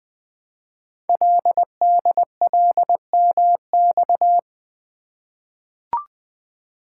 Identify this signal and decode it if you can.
Morse code, heard as a keyed tone, about 20 wpm, 700 Hz. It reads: LDLMX
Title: Code Groups - 5 Letters - 20wpm